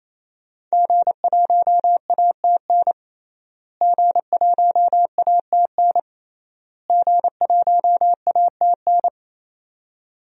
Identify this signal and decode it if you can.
Morse code, heard as a keyed tone, about 28 wpm, 700 Hz. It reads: G1ATN G1ATN G1ATN
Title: DXCC-100 Call Signs - 5 Characters - Repeated 3x - 28wpm